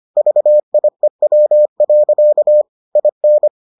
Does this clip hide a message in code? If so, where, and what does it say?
Yes: Morse code from 0.2 to 3.5 s, heard as a keyed tone, about 25 wpm, 600 Hz. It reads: VIEW. IN